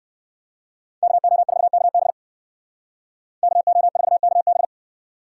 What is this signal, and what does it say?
Morse code, heard as a keyed tone, about 59 wpm, 700 Hz. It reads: KO4KB KO4KB